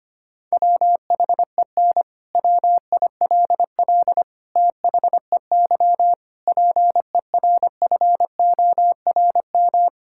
Morse code, heard as a keyed tone, about 25 wpm, 700 Hz. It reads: WHEN WILL THEY PERFORM
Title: Sentences from Top 1000 Words - Review Part 5 - 25wpm